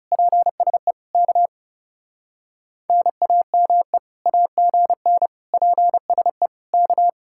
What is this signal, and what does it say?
Morse code, heard as a keyed tone, about 38 wpm, 700 Hz. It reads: PSE K NAME AGN PSE K